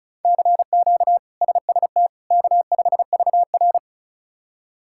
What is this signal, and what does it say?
Morse code, heard as a keyed tone, about 36 wpm, 700 Hz. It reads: CQ SST K5VR
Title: K1USN SST QSO Practice - 35wpm